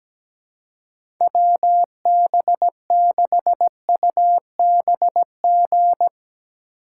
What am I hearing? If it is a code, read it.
Morse code, heard as a keyed tone, about 17 wpm, 700 Hz. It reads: WB6UBG
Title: USA Call Signs - 6 Characters - 3x Character Spacing - 17wpm